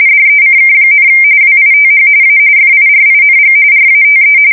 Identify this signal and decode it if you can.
RTTY, about 45.45 baud. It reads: DECK, WITH A FIRING RATE
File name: fsk170 / battleshipnj